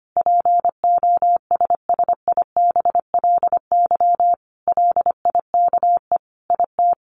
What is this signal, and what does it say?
Morse code, heard as a keyed tone, about 25 wpm, 700 Hz. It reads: POSSIBLY LIKE IT
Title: Sentences from Top 1300 Words 25wpm